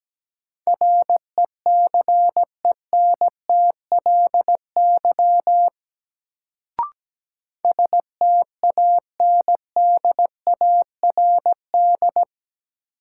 Morse code, heard as a keyed tone, about 17 wpm, 700 Hz. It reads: RECENTLY STANDARD
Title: Sentences from Top 1200 Words - New Word Drill Alt - 17wpm